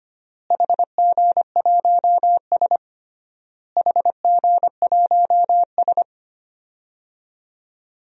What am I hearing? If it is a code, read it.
Morse code, heard as a keyed tone, about 25 wpm, 700 Hz. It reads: HG1S HG1S
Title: DXCC-100 Call Signs - 4 Characters - Repeated 3x - 25wpm